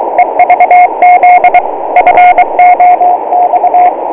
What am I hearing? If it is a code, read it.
Morse code, heard as a keyed tone, about 22 wpm, 720 Hz, with radio noise. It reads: EVZ FOX